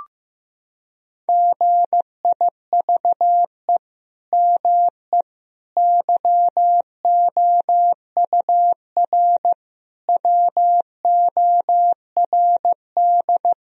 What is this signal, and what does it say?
Morse code, heard as a keyed tone, about 15 wpm, 700 Hz. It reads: GIVE ME YOUR WORD
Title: Sentences from Top 1000 Words - Review Part 3 - 15wpm